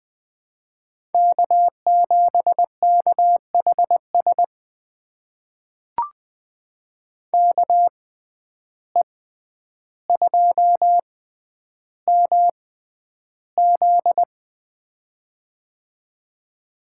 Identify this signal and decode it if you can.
Morse code, heard as a keyed tone, about 21 wpm, spaced as for 10 wpm, 700 Hz. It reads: K7KHS KE2MZ
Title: USA Call Signs - 5 Characters - 6x Character Spacing - 20wpm